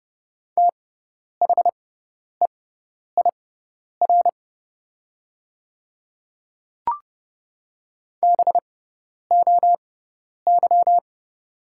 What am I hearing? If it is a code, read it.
Morse code, heard as a keyed tone, about 30 wpm, spaced as for 10 wpm, 700 Hz. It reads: THEIR BOY
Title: 6x Character Spacing - Top 300 Words - 30wpm